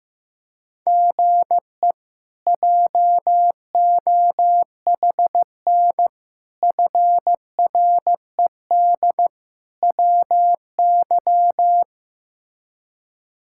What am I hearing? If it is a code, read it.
Morse code, heard as a keyed tone, about 15 wpm, 700 Hz. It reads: GE JOHN FRED WY
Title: K1USN SST QSO Practice - 15wpm